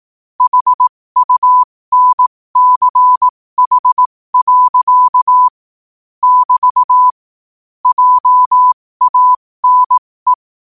Morse code, heard as a keyed tone, about 18 wpm, spaced as for 15 wpm, 1000 Hz. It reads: HUNCH. <BT> JANE
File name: qotdfiles / 15wpm